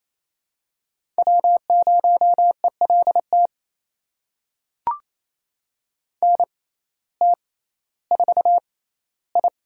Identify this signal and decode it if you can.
Morse code, heard as a keyed tone, about 28 wpm, spaced as for 15 wpm, 700 Hz. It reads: W0ELT NT4I
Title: USA Call Signs - 6x Character Spacing - 28wpm